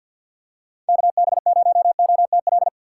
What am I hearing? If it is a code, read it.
Morse code, heard as a keyed tone, about 51 wpm, 700 Hz. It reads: KB0OTL
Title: USA Call Signs - 6 Characters - 4x Character Spacing - 50wpm